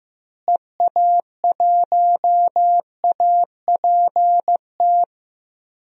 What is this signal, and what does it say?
Morse code, heard as a keyed tone, about 15 wpm, 700 Hz. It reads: EA1APT
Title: DXCC-100 Call Signs - 6 Characters - Repeated 3x - 15wpm